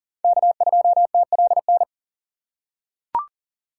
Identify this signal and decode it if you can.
Morse code, heard as a keyed tone, about 40 wpm, 700 Hz. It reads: K2TLN
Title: USA Call Signs - 5 Characters - 3x Character Spacing - 40wpm